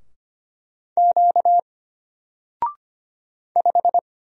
Morse code, heard as a keyed tone, about 25 wpm, 700 Hz. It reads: Q 5
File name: Single-Letter-Number-25wpm